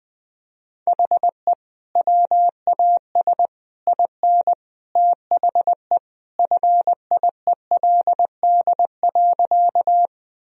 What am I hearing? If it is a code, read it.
Morse code, heard as a keyed tone, about 21 wpm, 700 Hz. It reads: HE WAS IN THE FIELD.
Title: Sentences from Top 1000 Words - Review Part 2 - 20wpm